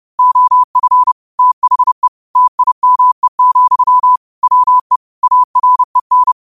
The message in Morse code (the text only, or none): OF THE TIME, WE AREN